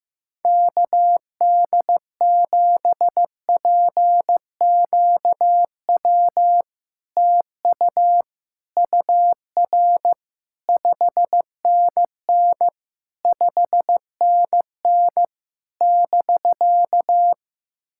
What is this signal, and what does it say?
Morse code, heard as a keyed tone, about 15 wpm, 700 Hz. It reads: KD7PQW TU UR 5NN 5NN <BK>
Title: SOTA QSO Practice 15wpm